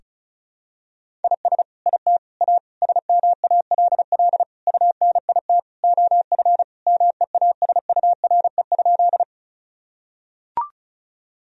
Morse code, heard as a keyed tone, about 36 wpm, 700 Hz. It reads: IS IT A SMALL UNIT OF MEASURE?
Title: Sentences from Top 1000 Words - Review Part 2 - 35wpm